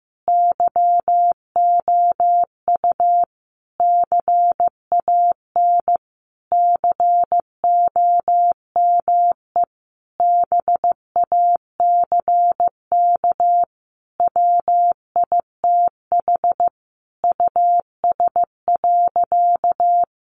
Morse code, heard as a keyed tone, about 15 wpm, 700 Hz. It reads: YOU CAN COME BACK WITH US.
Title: A Sound Of Thunder 15wpm